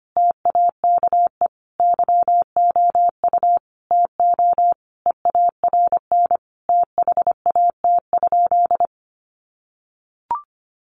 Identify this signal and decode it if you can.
Morse code, heard as a keyed tone, about 25 wpm, 700 Hz. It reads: TAKE YOU TO EARN THAT?
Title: Sentences from Top 1500 Words 25wpm